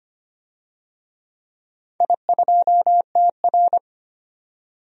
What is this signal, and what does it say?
Morse code, heard as a keyed tone, about 25 wpm, 700 Hz. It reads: I2TR
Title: DXCC-100 Call Signs - 4 Characters - 25wpm